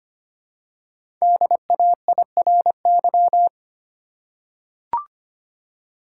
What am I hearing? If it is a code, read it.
Morse code, heard as a keyed tone, about 25 wpm, 700 Hz. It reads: DAIRY
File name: 5 Letter Words - First 3 Letters Given - 25wpm